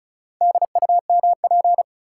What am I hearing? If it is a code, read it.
Morse code, heard as a keyed tone, about 36 wpm, 700 Hz. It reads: DUMP
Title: Sentences from Top 1300 Words - New Word Drill Alt - 35wpm